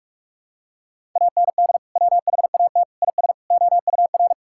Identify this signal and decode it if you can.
Morse code, heard as a keyed tone, about 46 wpm, 700 Hz. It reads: AND WHAT IS OUR
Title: Sentences from Top 1000 Words - Review Part 6 - 45wpm